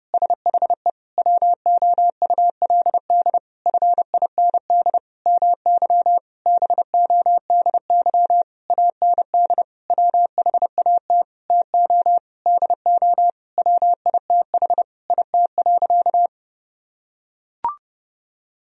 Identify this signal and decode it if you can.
Morse code, heard as a keyed tone, about 30 wpm, 700 Hz. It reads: SHE WOULD FIND MY BODY AND WHAT TO DO WITH IT.
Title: Sentences from Top 1000 Words - Review Part 2 - 30wpm